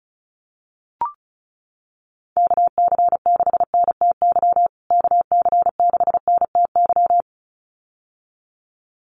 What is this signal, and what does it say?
Morse code, heard as a keyed tone, about 34 wpm, 700 Hz. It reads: KC6NTY KC6NTY
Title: USA Call Signs Repeated 2x 35wpm